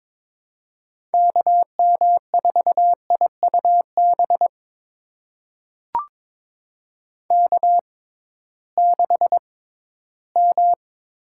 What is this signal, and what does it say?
Morse code, heard as a keyed tone, about 22 wpm, spaced as for 14 wpm, 700 Hz. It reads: KM4IUB K6M
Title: USA Call Signs - 6x Character Spacing - 22wpm